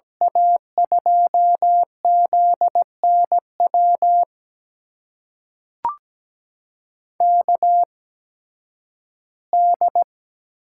Morse code, heard as a keyed tone, about 17 wpm, spaced as for 11 wpm, 700 Hz. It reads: A2ZNW KD
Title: USA Call Signs - 6 Characters - 8x Character Spacing - 17wpm